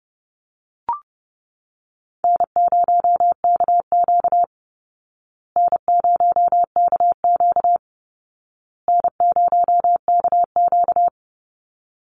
Morse code, heard as a keyed tone, about 30 wpm, 700 Hz. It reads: N0KQ N0KQ N0KQ
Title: Easy Call Signs Repeated 3x - 30wpm